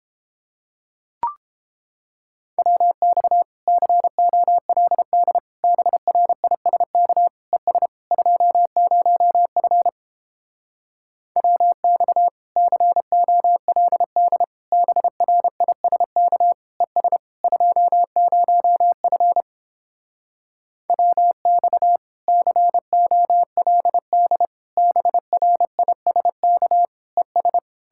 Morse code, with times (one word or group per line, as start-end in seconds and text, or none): 2.6-3.4 WX
3.7-5.4 COLD
5.6-7.3 BRISK
7.5-7.9 ES
8.1-9.9 20F
11.4-12.3 WX
12.6-14.4 COLD
14.7-16.5 BRISK
16.8-17.2 ES
17.4-19.4 20F
20.9-22.0 WX
22.3-24.5 COLD
24.8-26.9 BRISK
27.2-27.6 ES